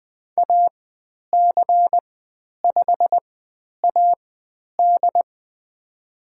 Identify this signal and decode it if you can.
Morse code, heard as a keyed tone, about 20 wpm, spaced as for 10 wpm, 700 Hz. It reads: AC5AD
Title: 109 Lesson 18g - Letter m - Call Signs 20wpm